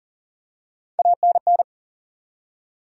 Morse code, heard as a keyed tone, about 40 wpm, 700 Hz. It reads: ANN
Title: USA Common Names - Max 4 Letters - 40wpm